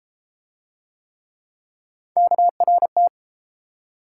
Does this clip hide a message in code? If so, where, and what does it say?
Yes: Morse code from 2.2 to 3.1 s, heard as a keyed tone, about 30 wpm, 700 Hz. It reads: KRT